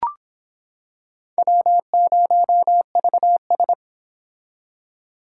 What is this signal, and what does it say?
Morse code, heard as a keyed tone, about 25 wpm, 700 Hz. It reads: W0VS